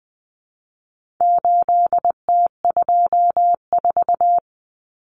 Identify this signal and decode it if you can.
Morse code, heard as a keyed tone, about 20 wpm, 700 Hz. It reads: 8T24
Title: Common Cut Numbers - 4-Digit 20wpm